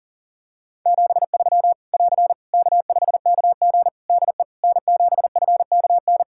Morse code, heard as a keyed tone, about 37 wpm, 700 Hz. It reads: 73 <AR> K5KG DE N7FKN
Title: K1USN SST QSO Practice 40wpm